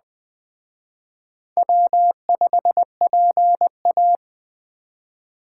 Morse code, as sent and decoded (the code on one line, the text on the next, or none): .-- ..... .--. .-
W5PA